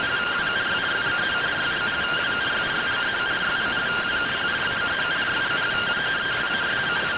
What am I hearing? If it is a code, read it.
Two-tone teleprinter signal, about 170 Hz shift.